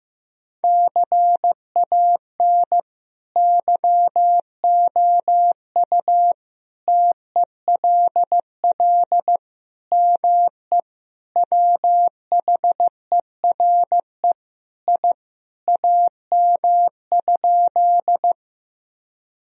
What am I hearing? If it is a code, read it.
Morse code, heard as a keyed tone, about 15 wpm, 700 Hz. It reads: CAN YOU TELL ME WHERE I AM?
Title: Sentences from Top 1000 Words - Review Part 2 - 15wpm